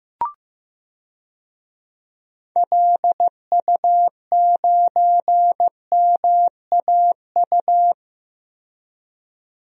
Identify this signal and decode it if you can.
Morse code, heard as a keyed tone, about 15 wpm, 700 Hz. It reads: LU9MAU
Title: DXCC-100 Call Signs - 6 Characters - 15wpm